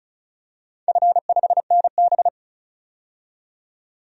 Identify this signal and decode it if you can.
Morse code, heard as a keyed tone, about 36 wpm, 700 Hz. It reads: F5NB
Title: DXCC-100 Call Signs - 4 Characters - 35wpm